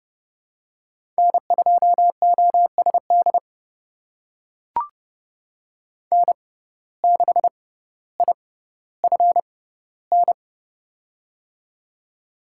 Morse code, heard as a keyed tone, about 30 wpm, spaced as for 15 wpm, 700 Hz. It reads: N2OSD N6IFN
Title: USA Call Signs - 6x Character Spacing - 30wpm